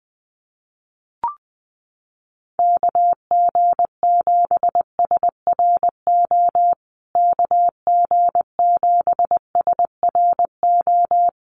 Morse code, heard as a keyed tone, about 20 wpm, 700 Hz. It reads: KG7SRO KG7SRO
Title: USA Call Signs Repeated 3x 20wpm